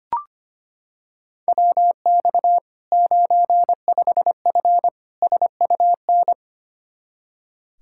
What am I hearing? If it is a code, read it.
Morse code, heard as a keyed tone, about 25 wpm, 700 Hz. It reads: WX 95F SUN
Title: Practice QSO - Exchange 2 - Info - 25wpm